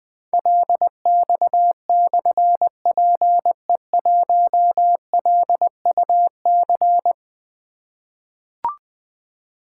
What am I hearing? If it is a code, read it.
Morse code, heard as a keyed tone, about 20 wpm, 700 Hz. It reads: LX/PE1LUC
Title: Difficult Call Signs 20wpm